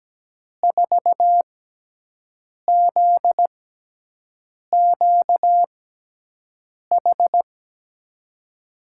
Morse code, heard as a keyed tone, about 17 wpm, spaced as for 6 wpm, 700 Hz. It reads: 4ZQH
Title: USA Call Signs - 6 Characters - 6x Character Spacing - 17wpm